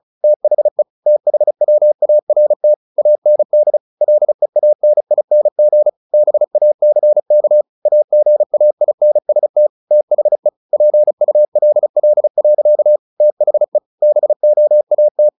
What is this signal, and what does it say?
Morse code, heard as a keyed tone, about 34 wpm, 600 Hz. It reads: THE THWART AND LEANING BACK AGAINST THE PULL. THE BOAT
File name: the old man and the sea / the old man and the sea 1 35wpm